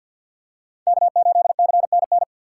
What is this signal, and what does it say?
Morse code, heard as a keyed tone, about 49 wpm, 690 Hz. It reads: K8CNN
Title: USA Call Signs - 5 Characters - 3x Character Spacing - 50wpm